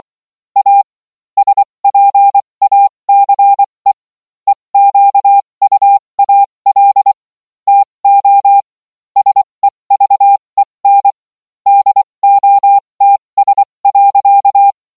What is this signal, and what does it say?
Morse code, heard as a keyed tone, about 24 wpm, spaced as for 20 wpm, 800 Hz. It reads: A SPACE EQUAL TO SEVEN DOTS.